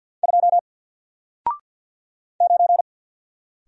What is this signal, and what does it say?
Morse code, heard as a keyed tone, about 54 wpm, 700 Hz. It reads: 2 9
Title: Single Character (Letter:Number:Punctuation:Prosign) 50wpm